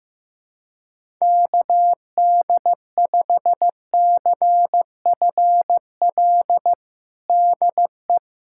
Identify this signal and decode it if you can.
Morse code, heard as a keyed tone, about 15 wpm, 700 Hz. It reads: KD5CFL DE
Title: USA Callsign DE Callsign 15wpm